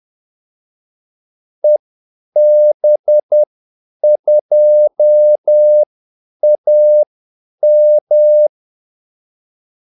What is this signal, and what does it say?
Morse code, heard as a keyed tone, about 10 wpm, spaced as for 8 wpm, 600 Hz. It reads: EB2AM
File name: Callsigns-10-8wpm